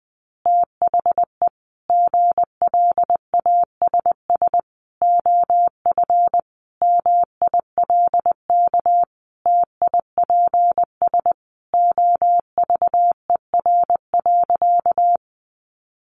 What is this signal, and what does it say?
Morse code, heard as a keyed tone, about 20 wpm, 700 Hz. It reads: THE GLASS OF MILK TIPS OVER.
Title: Additional Easy Phrases 20wpm